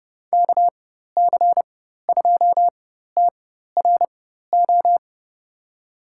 Morse code, heard as a keyed tone, about 32 wpm, spaced as for 14 wpm, 700 Hz. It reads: KC2TRO
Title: USA Call Signs - 6 Characters - 4x Character Spacing - 30wpm